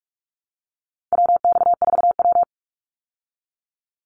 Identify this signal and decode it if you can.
Morse code, heard as a keyed tone, about 44 wpm, 700 Hz. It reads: WX4W
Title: Call Signs Repeated 3x - 45wpm